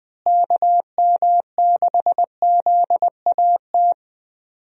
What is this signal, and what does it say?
Morse code, heard as a keyed tone, about 21 wpm, 700 Hz. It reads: KM6ZAT
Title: USA Call Signs - 6 Characters - Repeated 2x - 20wpm